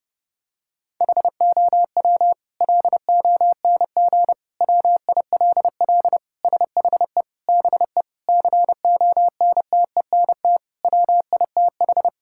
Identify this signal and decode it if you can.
Morse code, heard as a keyed tone, about 32 wpm, 700 Hz. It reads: HOW LONG WILL SHE BE CONTENT WITH
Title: Sentences from Top 1000 Words - Review Part 6 - 30wpm